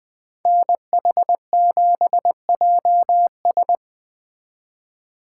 Morse code, heard as a keyed tone, about 20 wpm, 700 Hz. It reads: NH7JS
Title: Speed Racing - USA 2x2 Call Signs 17wpm